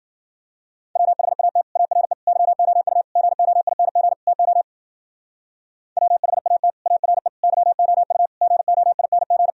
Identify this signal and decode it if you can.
Morse code, heard as a keyed tone, about 50 wpm, 700 Hz. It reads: WHAT ARE YOU GOING TO WHAT ARE YOU GOING